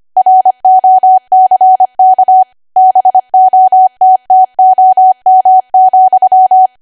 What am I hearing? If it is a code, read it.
Morse code, heard as a keyed tone, about 25 wpm, 750 Hz. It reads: ROCK BOTTOM,
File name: Archive / 25 WPM / 25WPM